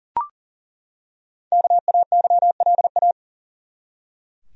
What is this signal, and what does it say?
Morse code, heard as a keyed tone, about 40 wpm, 700 Hz. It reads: KAYLA
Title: USA Common Names - 40wpm